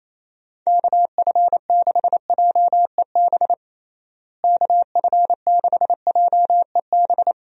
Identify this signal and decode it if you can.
Morse code, heard as a keyed tone, about 28 wpm, 700 Hz. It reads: KF6JEB KF6JEB
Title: USA Call Signs - 6 Characters - Repeated 2x - 28wpm